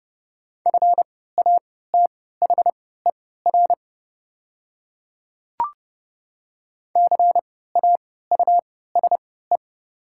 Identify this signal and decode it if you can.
Morse code, heard as a keyed tone, about 29 wpm, spaced as for 17 wpm, 700 Hz. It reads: FATHER CAUSE
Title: 3x Character Spacing - Top 200 Words - 30wpm